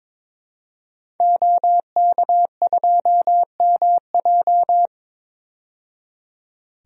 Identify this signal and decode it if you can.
Morse code, heard as a keyed tone, about 21 wpm, 700 Hz. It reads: OK2MJ
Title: DXCC-100 Call Signs - 5 Characters - 22wpm